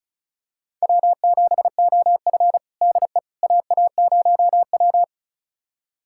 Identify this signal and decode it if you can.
Morse code, heard as a keyed tone, about 36 wpm, 700 Hz. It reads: W7OF DE AA0W
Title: USA Callsign DE Callsign - 4 Characters - 35wpm